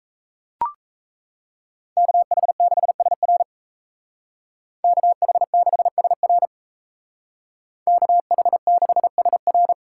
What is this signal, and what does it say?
Morse code, heard as a keyed tone, about 36 wpm, 700 Hz. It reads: KH6SR KH6SR KH6SR